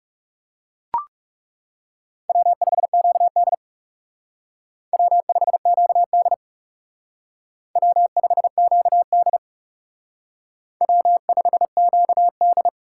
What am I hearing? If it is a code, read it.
Morse code, heard as a keyed tone, about 36 wpm, 700 Hz. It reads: W5QD W5QD W5QD W5QD
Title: Speed Racing - USA 1x2 Call Signs 30wpm